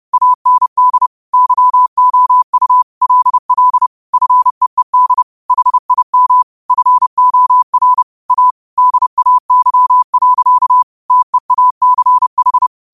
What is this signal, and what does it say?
Morse code, heard as a keyed tone, about 30 wpm, 1000 Hz. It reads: AND YOU LL FEED HIM FOR A DAY. TEACH
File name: qotdfiles / 30wpm